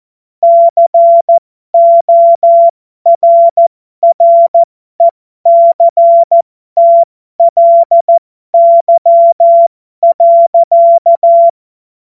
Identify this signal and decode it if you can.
Morse code, heard as a keyed tone, about 14 wpm, spaced as for 12 wpm, 680 Hz. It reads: CORRECTLY.